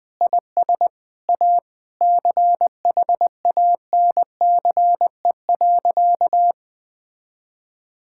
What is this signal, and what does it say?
Morse code, heard as a keyed tone, about 20 wpm, 700 Hz. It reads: IS A CHANCE.
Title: Sentences with 5-Word Phrases - 20wpm